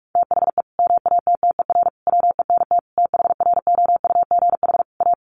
Morse code, heard as a keyed tone, about 44 wpm, 700 Hz. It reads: THE MATTER WENT THROUGH A
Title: Sentences from Top 1200 Words 45wpm